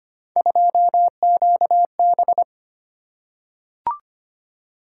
Morse code, heard as a keyed tone, about 25 wpm, 700 Hz. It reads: 2QB
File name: Three Letter-Number Combinations - 25wpm